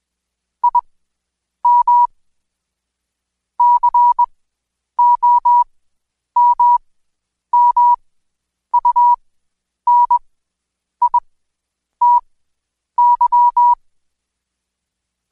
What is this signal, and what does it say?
Morse code, heard as a keyed tone, about 21 wpm, spaced as for 9 wpm, 970 Hz. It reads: IM COMMUNITY
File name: Code Quick / AudioDownld2 / Seg115-Day20-DayDrill 20